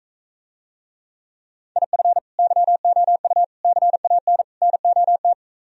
Morse code, heard as a keyed tone, about 45 wpm, 700 Hz. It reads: IF YOU CAN NOT